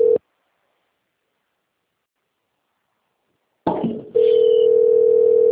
Ringback tone (North America), 440 + 480 Hz.